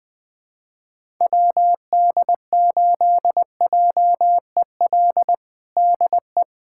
Morse code, heard as a keyed tone, about 20 wpm, 700 Hz. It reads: WD8JEL DE
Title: USA Callsign DE Callsign - 6 Characters - 20wpm